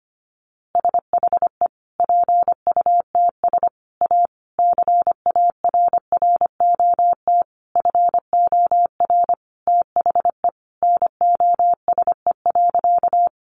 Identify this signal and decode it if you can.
Morse code, heard as a keyed tone, about 25 wpm, 700 Hz. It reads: SHE PUTS A CARROT FOR THE NOSE.
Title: Easy Phrases 25wpm